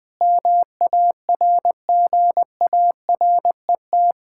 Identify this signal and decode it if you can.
Morse code, heard as a keyed tone, about 20 wpm, 700 Hz. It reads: MARGARET
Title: Practice QSO - Exchange 1 - Beginning - 20wpm